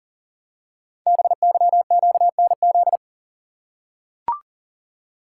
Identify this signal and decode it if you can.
Morse code, heard as a keyed tone, about 43 wpm, 700 Hz. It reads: BYQNZ
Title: Code Groups - 5 Letters - 40wpm